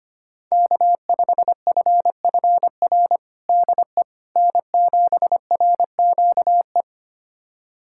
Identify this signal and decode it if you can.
Morse code, heard as a keyed tone, about 25 wpm, 700 Hz. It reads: K5FFR DE N7RQE
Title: USA Callsign DE Callsign - 5 Characters - 25wpm